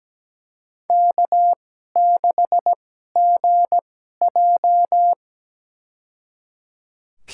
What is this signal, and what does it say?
Morse code, heard as a keyed tone, about 17 wpm, spaced as for 12 wpm, 700 Hz. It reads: K6GJ
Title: USA Call Signs - 4 Characters - 2x Character Spacing - 17wpm